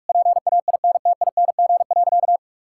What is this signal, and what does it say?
Morse code, heard as a keyed tone, about 41 wpm, 700 Hz. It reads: PAINTING.
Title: Sentences from Top 1000 Words - Review Part 6 - 45wpm